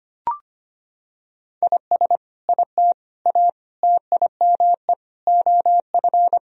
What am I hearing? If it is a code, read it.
Morse code, heard as a keyed tone, about 25 wpm, 700 Hz. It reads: IS IT A TIME OF
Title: Sentences from Top 1000 Words - Review Part 2 - 25wpm